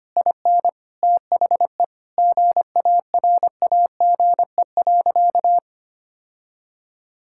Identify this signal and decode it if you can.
Morse code, heard as a keyed tone, about 25 wpm, 700 Hz. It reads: IN THE GARAGE.
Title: Sentences from Top 1600 Words 25wpm